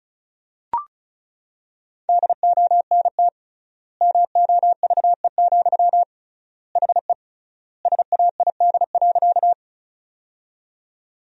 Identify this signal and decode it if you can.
Morse code, heard as a keyed tone, about 36 wpm, 700 Hz. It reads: DONT MOVE, HE SAID.
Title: 3x Word-Spacing - Sentences from Top 300 Words - 35wpm